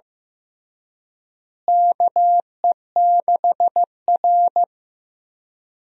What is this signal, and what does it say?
Morse code, heard as a keyed tone, about 16 wpm, 700 Hz. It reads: KE6R
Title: USA Call Signs - 4 Characters - Repeated 3x - 15wpm